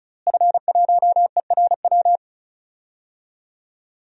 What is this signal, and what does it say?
Morse code, heard as a keyed tone, about 36 wpm, 700 Hz. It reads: F1ERW
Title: DXCC-100 Call Signs - 5 Characters - Repeated 2x - 35wpm